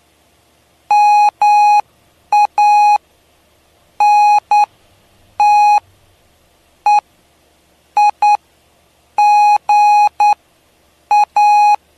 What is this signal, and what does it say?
Morse code, heard as a keyed tone, about 9 wpm, spaced as for 6 wpm, 830 Hz. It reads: MANTEIGA
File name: cw / py2lcd / 07-PY2LCD-Li��o 7